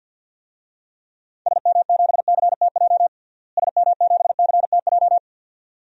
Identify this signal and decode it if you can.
Morse code, heard as a keyed tone, about 47 wpm, 700 Hz. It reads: SM7CTJ SM7CTJ
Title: DXCC-100 Call Signs - 6 Characters - Repeated 2x - 50wpm